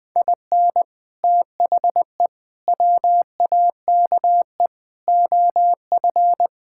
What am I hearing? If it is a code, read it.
Morse code, heard as a keyed tone, about 20 wpm, 700 Hz. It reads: IN THE WAKE OF
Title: Build A Phrase - 4-Word Phrases - 20wpm